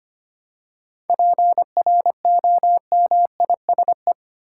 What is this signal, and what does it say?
Morse code, heard as a keyed tone, about 25 wpm, 700 Hz. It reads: PROMISE
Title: Seven Letter Words - 25wpm